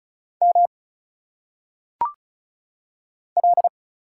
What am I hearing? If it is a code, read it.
Morse code, heard as a keyed tone, about 35 wpm, 700 Hz. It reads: M L
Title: Single Letters - Phonetic Pronunciation 35wpm